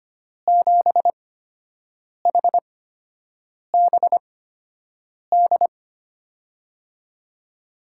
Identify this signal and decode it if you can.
Morse code, heard as a keyed tone, about 25 wpm, spaced as for 7 wpm, 700 Hz. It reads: 7HBD
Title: USA Call Signs - 8x Character Spacing - 25wpm